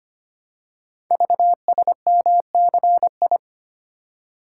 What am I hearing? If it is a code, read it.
Morse code, heard as a keyed tone, about 25 wpm, 700 Hz. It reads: VSMCI
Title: Code Groups - 5 Letter:Numbers - 25wpm